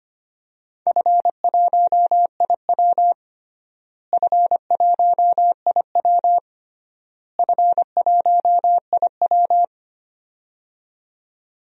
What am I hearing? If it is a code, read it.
Morse code, heard as a keyed tone, about 25 wpm, 700 Hz. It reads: F1IW F1IW F1IW
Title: DXCC-100 Call Signs - 4 Characters - Repeated 3x - 25wpm